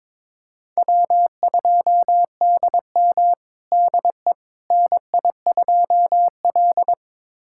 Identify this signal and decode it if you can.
Morse code, heard as a keyed tone, about 22 wpm, 700 Hz. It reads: W2DM DE NI2L
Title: USA Callsign DE Callsign - 4 Characters - 22wpm